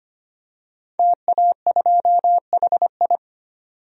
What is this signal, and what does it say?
Morse code, heard as a keyed tone, about 25 wpm, 700 Hz. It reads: TA2HI
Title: DXCC-100 Call Signs - 5 Characters - 25wpm